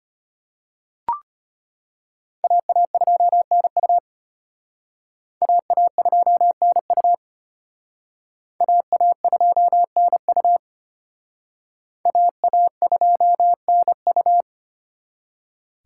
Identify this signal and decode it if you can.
Morse code, heard as a keyed tone, about 31 wpm, 700 Hz. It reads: AA2NU AA2NU AA2NU AA2NU
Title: Speed Racing - USA 2x2 Call Signs 25wpm